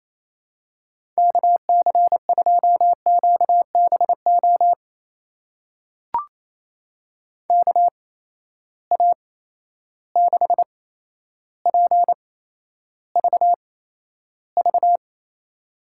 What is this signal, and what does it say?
Morse code, heard as a keyed tone, about 28 wpm, spaced as for 12 wpm, 700 Hz. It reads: KC2QBO KA6PVV
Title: USA Call Signs - 6 Characters - 8x Character Spacing - 28wpm